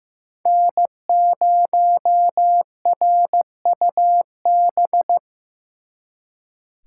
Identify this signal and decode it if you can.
Morse code, heard as a keyed tone, about 18 wpm, 700 Hz. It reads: N0RUB
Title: Speed Racing - USA Call Signs 15wpm